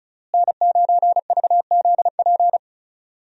Morse code, heard as a keyed tone, about 34 wpm, 700 Hz. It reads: N9VZP
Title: USA Call Signs - 5 Characters - Repeated 2x - 35wpm